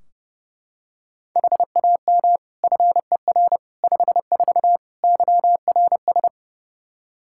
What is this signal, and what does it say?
Morse code, heard as a keyed tone, about 32 wpm, 700 Hz. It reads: HAM FER 54 YRS
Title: K1USN SST QSO Practice 30wpm